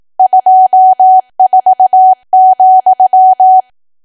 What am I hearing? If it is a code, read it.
Morse code, heard as a keyed tone, about 18 wpm, 750 Hz. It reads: 24,
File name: cwa / W1AW-Level2 / 18WPM